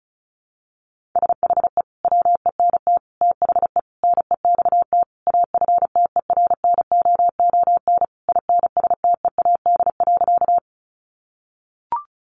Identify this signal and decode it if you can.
Morse code, heard as a keyed tone, about 34 wpm, 700 Hz. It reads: SHE WENT THE NEXT AFTERNOON INSTEAD.
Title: Sentences from Top 1500 Words 35wpm